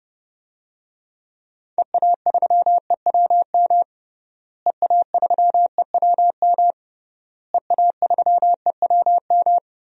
Morse code, heard as a keyed tone, about 30 wpm, 700 Hz. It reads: EA3EWM EA3EWM EA3EWM
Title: DXCC-100 Call Signs - 6 Characters - Repeated 3x - 30wpm